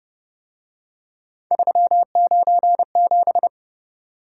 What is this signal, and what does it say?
Morse code, heard as a keyed tone, about 31 wpm, 700 Hz. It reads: 397